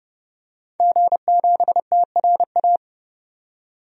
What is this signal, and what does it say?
Morse code, heard as a keyed tone, about 30 wpm, 700 Hz. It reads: G7TRA
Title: DXCC-100 Call Signs - 5 Characters - 30wpm